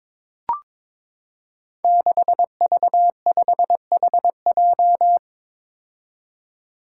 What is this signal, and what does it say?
Morse code, heard as a keyed tone, about 22 wpm, 700 Hz. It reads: 6V5HJ
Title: Five Challenging Letter-Number Combinations 22wpm